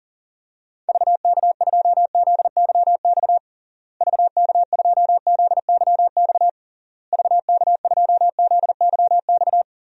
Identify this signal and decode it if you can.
Morse code, heard as a keyed tone, about 40 wpm, 700 Hz. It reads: VK2ZYX VK2ZYX VK2ZYX
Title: DXCC-100 Call Signs - 6 Characters - Repeated 3x - 40wpm